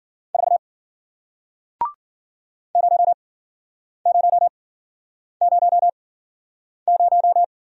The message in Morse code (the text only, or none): 4 0000